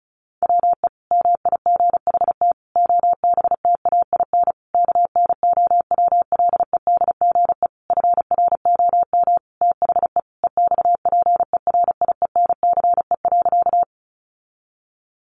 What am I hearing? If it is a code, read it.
Morse code, heard as a keyed tone, about 34 wpm, 700 Hz. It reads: WE MIGHT OBTAIN KNOWLEDGE FROM THE EXPERIENCE.
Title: Sentences from Top 1300 Words 35wpm